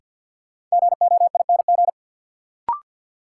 Morse code, heard as a keyed tone, about 53 wpm, 700 Hz. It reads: GOING
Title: Sets of 2 Words - New Word Drill - 50wpm